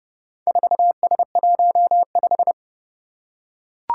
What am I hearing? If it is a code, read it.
Morse code, heard as a keyed tone, about 32 wpm, 700 Hz. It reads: 4S15
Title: Four Challenging Letter-Number Combinations 30wpm